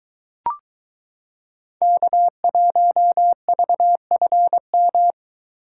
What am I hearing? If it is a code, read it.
Morse code, heard as a keyed tone, about 23 wpm, 700 Hz. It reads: K1VFM